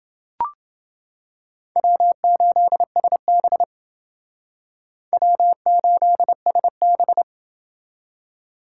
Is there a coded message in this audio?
Yes: Morse code from 1.8 to 7.2 s, heard as a keyed tone, about 29 wpm, 700 Hz. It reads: W8SB W8SB